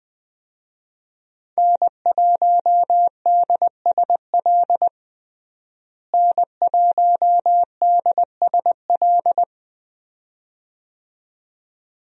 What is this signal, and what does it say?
Morse code, heard as a keyed tone, about 20 wpm, 700 Hz. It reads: N1DSL N1DSL
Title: USA Call Signs - 5 Characters - Repeated 2x - 20wpm